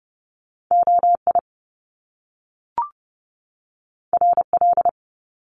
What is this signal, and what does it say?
Morse code, heard as a keyed tone, about 30 wpm, 700 Hz. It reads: O I R L
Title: Top Two Letter Combinations in Words 30wpm